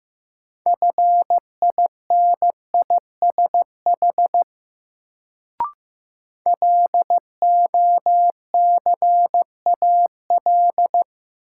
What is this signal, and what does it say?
Morse code, heard as a keyed tone, about 15 wpm, 700 Hz. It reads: FINISH LOCAL
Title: Sentences from Top 1000 Words - Review Part 3 - New Word Drill - 15wpm